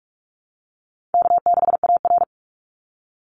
Morse code, heard as a keyed tone, about 44 wpm, 700 Hz. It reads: K6AR
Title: Easy Call Signs - 45wpm